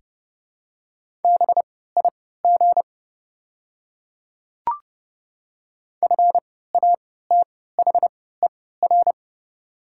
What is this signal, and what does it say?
Morse code, heard as a keyed tone, about 29 wpm, spaced as for 17 wpm, 690 Hz. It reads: BIG FATHER
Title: 3x Character Spacing - Top 200 Words - 30wpm